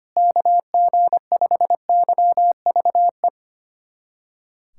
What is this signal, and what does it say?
Morse code, heard as a keyed tone, about 25 wpm, 700 Hz. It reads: KG5YVE
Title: USA Callsign DE Callsign - 6 Characters - 25wpm